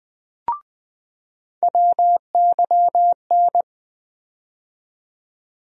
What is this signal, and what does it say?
Morse code, heard as a keyed tone, about 20 wpm, 700 Hz. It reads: WYN
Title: USA Call Sign Postfixes - 3 Letters 20wpm